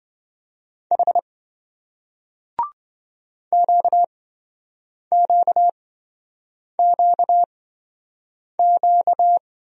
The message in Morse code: H QQQQ

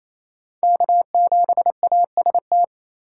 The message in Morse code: K7AST

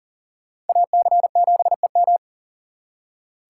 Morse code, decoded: AC7EM

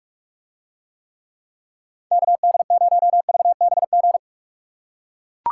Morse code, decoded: KD0VBG

